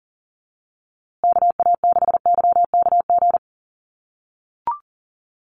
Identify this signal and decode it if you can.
Morse code, heard as a keyed tone, about 40 wpm, 700 Hz. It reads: KA6YKG
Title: USA Call Signs Repeated 2x 40wpm